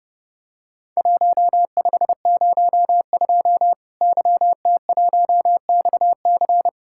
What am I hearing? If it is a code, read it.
Morse code, heard as a keyed tone, about 30 wpm, 700 Hz. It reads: 1502 YT1XC
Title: Worked All Europe DX Contest - QTC Exchange - 30wpm